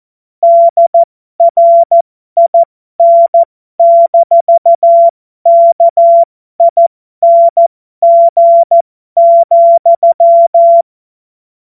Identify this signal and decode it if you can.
Morse code, heard as a keyed tone, about 14 wpm, spaced as for 12 wpm, 680 Hz. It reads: DRIN-KING,
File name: Morsekurs / Witze